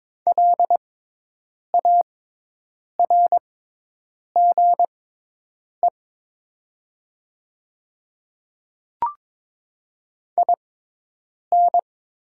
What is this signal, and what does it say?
Morse code, heard as a keyed tone, about 22 wpm, spaced as for 8 wpm, 700 Hz. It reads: LARGE IN
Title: 6x Character Spacing - Top 200 Words - 22wpm